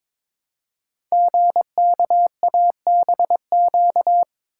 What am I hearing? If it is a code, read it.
Morse code, heard as a keyed tone, about 23 wpm, 700 Hz. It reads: GKABQ
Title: Code Groups - 5 Letters - 22wpm